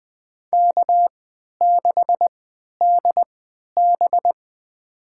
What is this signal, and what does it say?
Morse code, heard as a keyed tone, about 20 wpm, spaced as for 11 wpm, 700 Hz. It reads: K6DB
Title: USA Call Signs - 4 Characters - 3x Character Spacing - 20wpm